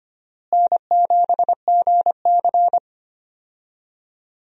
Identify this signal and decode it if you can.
Morse code, heard as a keyed tone, about 25 wpm, 700 Hz. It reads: N7GC
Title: USA Call Signs - 4 Characters - Repeated 2x - 25wpm